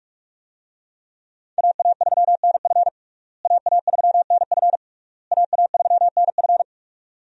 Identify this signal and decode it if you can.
Morse code, heard as a keyed tone, about 41 wpm, 700 Hz. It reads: AA3NF AA3NF AA3NF
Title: USA Call Signs - 5 Characters - Repeated 3x - 45wpm